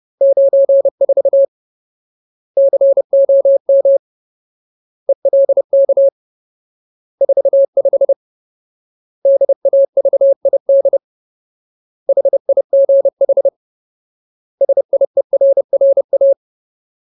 Morse code, heard as a keyed tone, about 30 wpm, 550 Hz. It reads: 94 COM ELK 45 DAVID HIGH SIERRA
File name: Ultimate Word List / 30WPM Ultimate